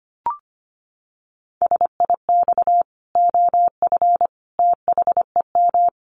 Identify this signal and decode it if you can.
Morse code, heard as a keyed tone, about 25 wpm, 700 Hz. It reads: SIX OF THEM
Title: Sentences from Top 1100 Words 25wpm